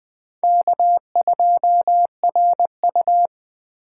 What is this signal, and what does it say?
Morse code, heard as a keyed tone, about 20 wpm, 700 Hz. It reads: K2RU
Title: Speed Racing - USA 1x2 Call Signs 17wpm